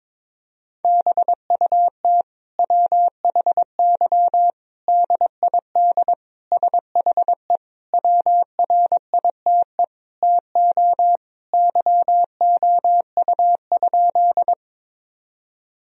Morse code, heard as a keyed tone, about 21 wpm, 700 Hz. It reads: BUT WHY DID SHE WRITE TO YOU?
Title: Sentences from Top 1000 Words - Review Part 1 - 22wpm